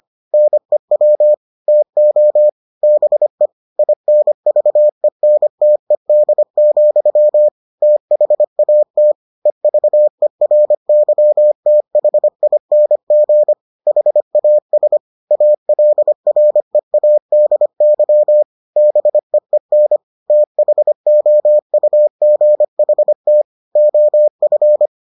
Morse code, heard as a keyed tone, about 25 wpm, 600 Hz. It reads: NEW TO BE INVENTED, THAT EVERYTHING HAS ALREADY BEEN THOUGHT OF